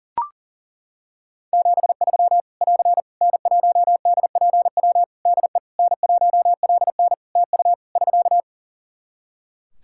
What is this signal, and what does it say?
Morse code, heard as a keyed tone, about 41 wpm, 700 Hz. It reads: 73 <AR> N1DPW DE N1LN TU <SK>
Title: Practice QSO - Exchange 4 - Ending - 40wpm